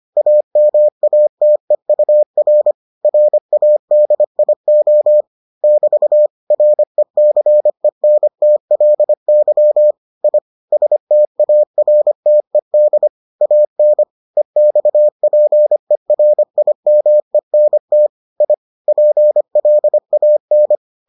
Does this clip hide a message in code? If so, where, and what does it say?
Yes: Morse code from 0.2 to 20.8 s, heard as a keyed tone, about 25 wpm, 600 Hz. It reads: AMATEUR RADIO <BT> RECENTLY I STARTED AN EXPERIMENT I PLAN